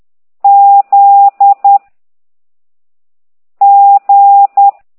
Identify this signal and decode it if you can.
Morse code, heard as a keyed tone, about 10 wpm, 800 Hz. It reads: Z G